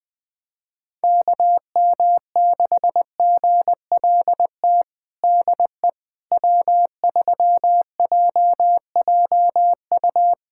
Morse code, heard as a keyed tone, about 20 wpm, 700 Hz. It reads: KM6GLT DE W3JJU
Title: USA Callsign DE Callsign 20wpm